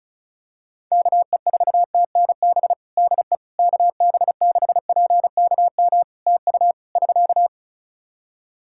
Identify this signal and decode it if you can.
Morse code, heard as a keyed tone, about 36 wpm, 700 Hz. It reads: KE4TNB DE KB6PKM TU <SK>
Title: Practice QSO - Exchange 4 - Ending - 35wpm